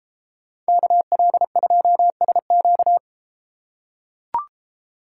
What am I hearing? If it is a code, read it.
Morse code, heard as a keyed tone, about 31 wpm, 700 Hz. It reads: KL2SQ